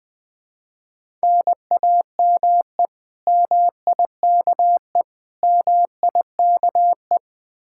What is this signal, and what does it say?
Morse code, heard as a keyed tone, about 21 wpm, 700 Hz. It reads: NAME MIKE MIKE